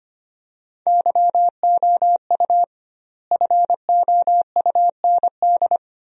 Morse code, heard as a keyed tone, about 25 wpm, 700 Hz. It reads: YOU FOUND
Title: 2x Word-Spacing - Sentences from Top 300 Words - 25wpm